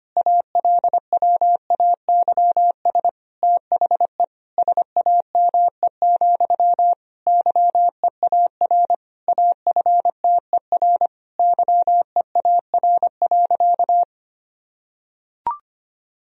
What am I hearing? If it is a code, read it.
Morse code, heard as a keyed tone, about 25 wpm, 700 Hz. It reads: ALWAYS THE SAME, YEAR AFTER YEAR.
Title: Sentences from Top 1000 Words - Review Part 1 - 25wpm